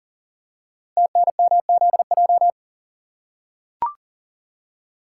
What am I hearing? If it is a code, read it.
Morse code, heard as a keyed tone, about 43 wpm, 700 Hz. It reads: TNMZJ
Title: Code Groups - 5 Letters - 40wpm